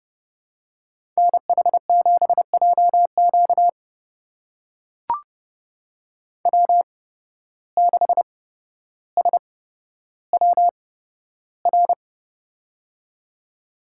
Morse code, heard as a keyed tone, about 30 wpm, spaced as for 13 wpm, 700 Hz. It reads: NH7JQ W6SWR
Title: USA Call Signs - 5 Characters - 8x Character Spacing - 30wpm